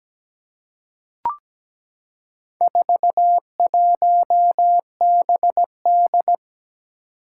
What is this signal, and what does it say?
Morse code, heard as a keyed tone, about 17 wpm, 700 Hz. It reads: 41BD
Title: Four Challenging Letter-Number Combinations 17wpm